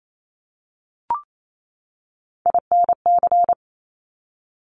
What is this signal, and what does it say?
Morse code, heard as a keyed tone, about 28 wpm, 690 Hz. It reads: INC
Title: Top Three Letter Combinations in Words 28wpm